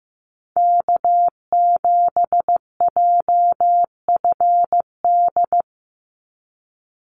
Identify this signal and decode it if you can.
Morse code, heard as a keyed tone, about 15 wpm, 700 Hz. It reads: K7JFD
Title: NA SOTA Activator Callsigns 15wpm